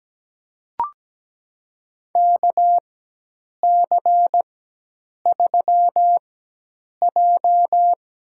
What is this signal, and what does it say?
Morse code, heard as a keyed tone, about 17 wpm, spaced as for 8 wpm, 700 Hz. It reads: KC3J
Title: USA Call Signs - 6 Characters - 4x Character Spacing - 17wpm